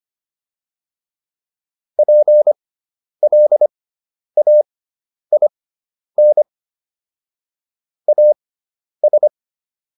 Morse code, heard as a keyed tone, about 25 wpm, spaced as for 10 wpm, 600 Hz. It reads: PLAIN AS